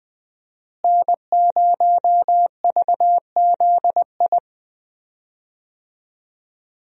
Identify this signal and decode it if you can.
Morse code, heard as a keyed tone, about 21 wpm, 700 Hz. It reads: N0VZI
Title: USA Call Signs - 5 Characters - Repeated 2x - 20wpm